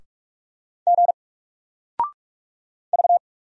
Morse code, heard as a keyed tone, about 44 wpm, 700 Hz. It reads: G V
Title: Single Character (Letter:Number:Punctuation:Prosign) 45wpm